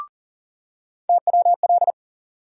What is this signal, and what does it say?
Morse code, heard as a keyed tone, about 40 wpm, 700 Hz. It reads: TWL
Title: USA Call Sign Postfixes - 3 Letters 40wpm